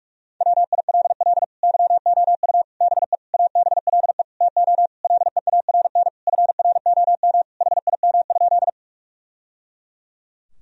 Morse code, heard as a keyed tone, about 46 wpm, 700 Hz. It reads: WILL YOU BE ABLE TO LEARN FROM HIM?
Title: Sentences from Top 1000 Words - Review Part 2 - 45wpm